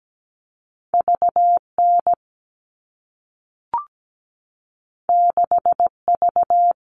Morse code, heard as a keyed tone, about 17 wpm, 700 Hz. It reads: VN 6V
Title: All Cut Numbers - 2-Digit 17wpm